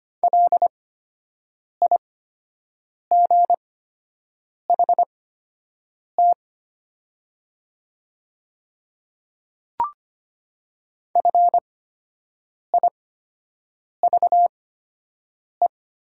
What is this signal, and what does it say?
Morse code, heard as a keyed tone, about 25 wpm, spaced as for 7 wpm, 700 Hz. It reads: LIGHT FIVE
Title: 8x Character Spacing - Top 500 Words - 25wpm